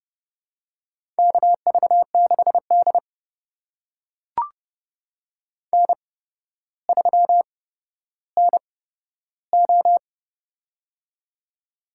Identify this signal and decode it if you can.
Morse code, heard as a keyed tone, about 30 wpm, spaced as for 13 wpm, 700 Hz. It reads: KV6D N3NO
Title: USA Call Signs - 4 Characters - 8x Character Spacing - 30wpm